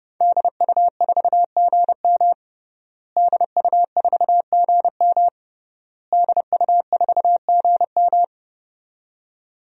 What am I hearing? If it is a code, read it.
Morse code, heard as a keyed tone, about 30 wpm, 700 Hz. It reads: DU4GM DU4GM DU4GM
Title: DXCC-100 Call Signs - 5 Characters - Repeated 3x - 30wpm